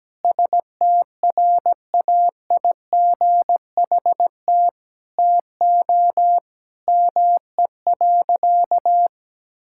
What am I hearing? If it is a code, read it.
Morse code, heard as a keyed tone, about 17 wpm, 700 Hz. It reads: STRAIGHT TO ME.
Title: Sentences from Top 1000 Words - Review Part 6 - 17wpm